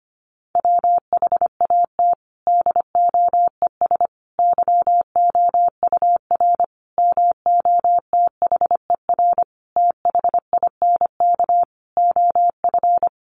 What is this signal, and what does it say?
Morse code, heard as a keyed tone, about 25 wpm, 700 Hz. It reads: WHAT DOES YOUR MOTHER THINK OF
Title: Sentences from Top 1400 Words 25wpm